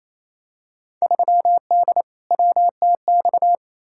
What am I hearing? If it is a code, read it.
Morse code, heard as a keyed tone, about 28 wpm, 700 Hz. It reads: 3D WTX
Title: Field Day Exchanges Encore - 28wpm